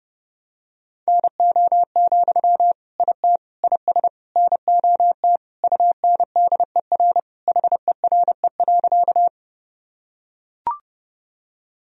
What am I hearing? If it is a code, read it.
Morse code, heard as a keyed tone, about 30 wpm, 700 Hz. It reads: NO, IT IS NOT UNDER HERE.
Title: Sentences from Top 1000 Words - Review Part 1 - 30wpm